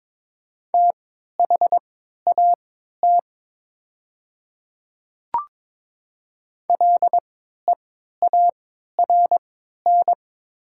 Morse code, heard as a keyed tone, about 22 wpm, spaced as for 13 wpm, 700 Hz. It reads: THAT LEARN
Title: 3x Character Spacing - Top 300 Words - 22wpm